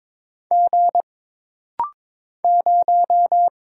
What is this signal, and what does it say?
Morse code, heard as a keyed tone, about 22 wpm, 700 Hz. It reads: G 0
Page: This rapid-fire timing is geared to push your ICR (Instant Character Recognition) skills.